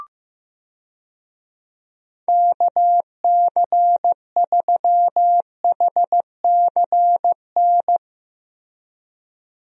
Morse code, heard as a keyed tone, about 15 wpm, 700 Hz. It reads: KC3HCN
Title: USA Call Signs - 6 Characters - Repeated 2x - 15wpm